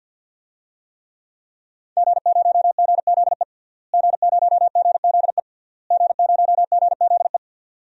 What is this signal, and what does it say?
Morse code, heard as a keyed tone, about 46 wpm, 700 Hz. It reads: G0GZE G0GZE G0GZE
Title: DXCC-100 Call Signs - 5 Characters - Repeated 3x - 50wpm